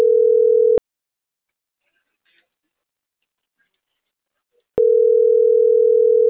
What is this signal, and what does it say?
Ringback tone (North America), 440 + 480 Hz.